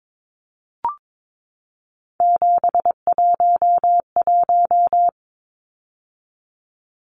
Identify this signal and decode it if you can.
Morse code, heard as a keyed tone, about 22 wpm, 700 Hz. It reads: 711
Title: Numbers - 3-Digit 22wpm